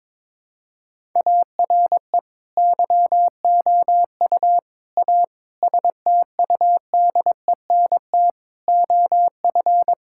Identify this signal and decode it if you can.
Morse code, heard as a keyed tone, about 22 wpm, 700 Hz. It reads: ARE YOU A STUDENT OF
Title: Sentences from Top 1000 Words - Review Part 6 - 22wpm